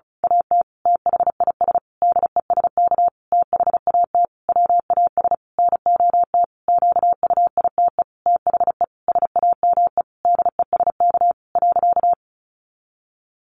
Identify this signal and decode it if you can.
Morse code, heard as a keyed tone, about 34 wpm, 700 Hz. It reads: AT THIS DESK THAT WAS NOT QUITE THE SAME DESK .
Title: A Sound Of Thunder 35wpm